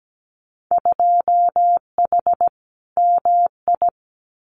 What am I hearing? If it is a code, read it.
Morse code, heard as a keyed tone, about 17 wpm, 700 Hz. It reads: 2H MI
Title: Winter Field Day Exchanges Encore 17wpm